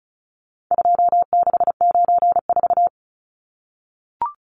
Morse code, heard as a keyed tone, about 34 wpm, 700 Hz. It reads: 2694
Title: Numbers - 4-Digit 35wpm